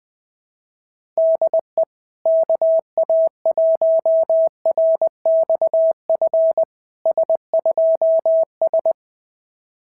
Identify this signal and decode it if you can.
Morse code, heard as a keyed tone, about 21 wpm, 690 Hz. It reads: DE KA1RXF S2S
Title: SOTA QSO Practice 20wpm